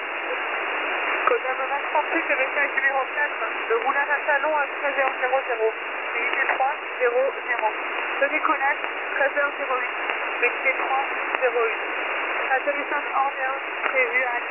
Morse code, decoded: none

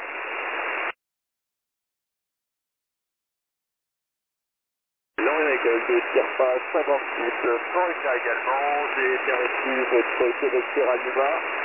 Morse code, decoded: none

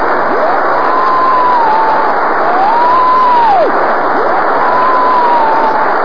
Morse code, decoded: none

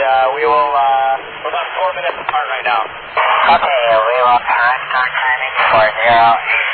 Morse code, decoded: none

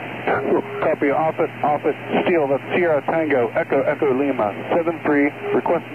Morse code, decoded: none

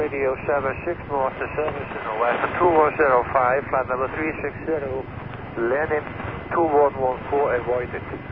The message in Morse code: none

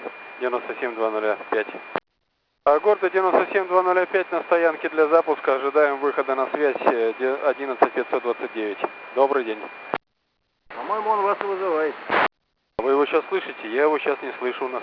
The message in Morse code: none

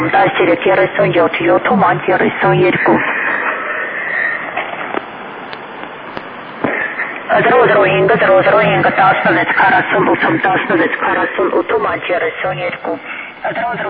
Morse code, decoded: none

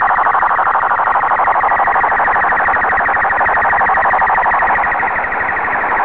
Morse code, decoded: none

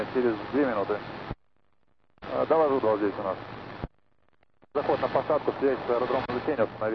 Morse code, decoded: none

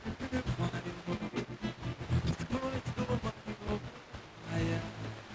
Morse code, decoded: none

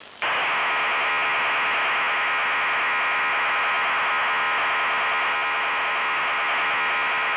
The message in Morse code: none

